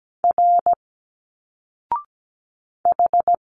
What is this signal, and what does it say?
Morse code, heard as a keyed tone, about 17 wpm, 700 Hz. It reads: R H
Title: Single Letters 17wpm